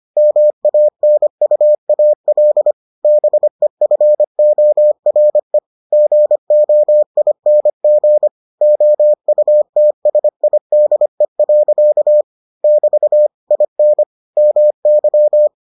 Morse code, heard as a keyed tone, about 25 wpm, 600 Hz. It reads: MANUAL BEFORE GOING OUTSIDE. <BT> IN MY